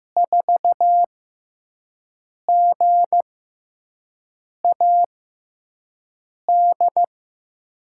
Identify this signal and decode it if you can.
Morse code, heard as a keyed tone, about 15 wpm, spaced as for 5 wpm, 700 Hz. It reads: 4GAD
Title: USA Call Signs - 6x Character Spacing - 15wpm